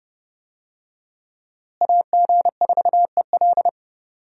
Morse code, heard as a keyed tone, about 32 wpm, 700 Hz. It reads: AG4EL
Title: USA Call Signs - 5 Characters - Repeated 3x - 30wpm